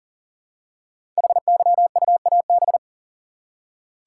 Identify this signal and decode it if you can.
Morse code, heard as a keyed tone, about 40 wpm, 700 Hz. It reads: HYUAB
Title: Code Groups - 5 Letter:Numbers - 40wpm